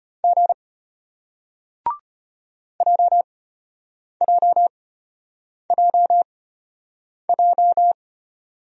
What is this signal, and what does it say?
Morse code, heard as a keyed tone, about 34 wpm, spaced as for 8 wpm, 700 Hz. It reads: G JJJJ